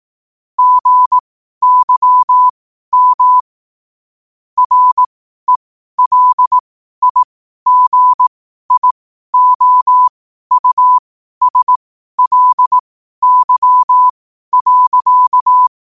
Morse code, heard as a keyed tone, about 18 wpm, spaced as for 12 wpm, 1000 Hz. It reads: GYM RELIGIOUSLY.